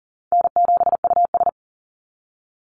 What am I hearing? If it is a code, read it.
Morse code, heard as a keyed tone, about 40 wpm, 700 Hz. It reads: N7US
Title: Call Signs Repeated 3x - 40wpm